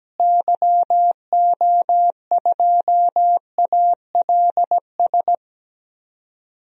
Morse code, heard as a keyed tone, about 17 wpm, 700 Hz. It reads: YO2ALS
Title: DXCC-100 Call Signs - 6 Characters - Repeated 3x - 17wpm